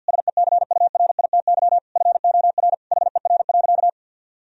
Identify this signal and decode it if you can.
Morse code, heard as a keyed tone, about 52 wpm, 700 Hz. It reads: SECURITY FOR HER.
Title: Sentences from Top 1000 Words - Review Part 4 - 50wpm